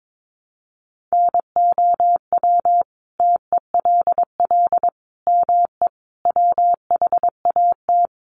Morse code, heard as a keyed tone, about 22 wpm, 700 Hz. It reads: NOW TELL ME WHAT
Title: Sentences from Top 1300 Words 22wpm